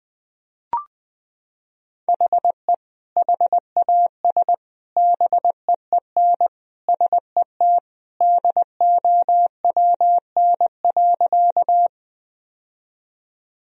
Morse code, heard as a keyed tone, about 21 wpm, 700 Hz. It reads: HE HAS BEEN SET DOWN.
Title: Sentences from Top 1000 Words - Review Part 1 - 20wpm